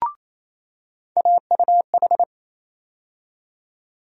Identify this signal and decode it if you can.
Morse code, heard as a keyed tone, about 28 wpm, 700 Hz. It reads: AUH
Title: USA Call Sign Postfixes - 3 Letters 28wpm